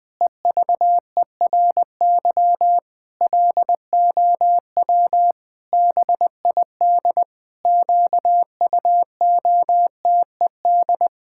Morse code, heard as a keyed tone, about 19 wpm, 700 Hz. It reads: EVERY LOW BID QUOTED
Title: Pangram Challenge v2 - 20wpm